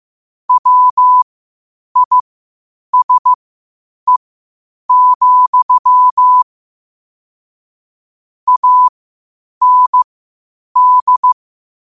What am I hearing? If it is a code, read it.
Morse code, heard as a keyed tone, about 15 wpm, spaced as for 8 wpm, 1000 Hz. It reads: WISE, AND
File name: qotdfiles / 8wpm